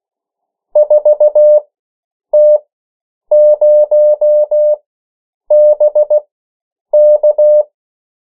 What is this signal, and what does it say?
Morse code, heard as a keyed tone, about 16 wpm, spaced as for 8 wpm, 600 Hz. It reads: 4T0BK
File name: morse mp3 / 14 urok 13